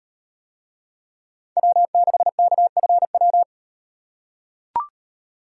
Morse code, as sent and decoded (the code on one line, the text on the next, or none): .-- -.... -.- ..-. .--
W6KFW